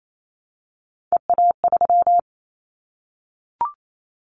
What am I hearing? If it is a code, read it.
Morse code, heard as a keyed tone, about 28 wpm, 700 Hz. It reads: EA3
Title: Callsign Prefixes for Top 25 DXCC - Repeated 3x - 28wpm